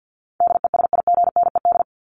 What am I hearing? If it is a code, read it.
Morse code, heard as a keyed tone, about 52 wpm, 700 Hz. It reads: DESIGNED
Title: ED Suffix - Top 100 Words with Base Words 50wpm